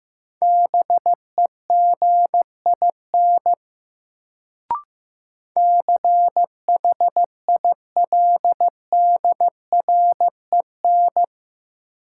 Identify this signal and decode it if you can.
Morse code, heard as a keyed tone, about 15 wpm, 700 Hz. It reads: BEGIN CHILDREN
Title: Sentences from Top 300 Words - New Word Drill - 15wpm